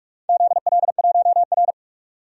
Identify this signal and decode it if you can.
Morse code, heard as a keyed tone, about 47 wpm, 700 Hz. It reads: ZL1R
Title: DXCC-100 Call Signs - 4 Characters - 45wpm